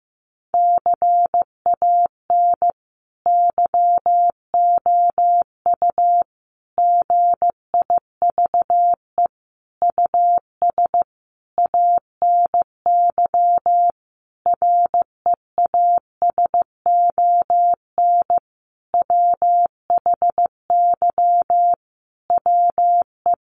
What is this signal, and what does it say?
Morse code, heard as a keyed tone, about 15 wpm, 700 Hz. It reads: CAN YOU GIVE US ANY REASON WHY WE
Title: Sentences from Top 1100 Words 15wpm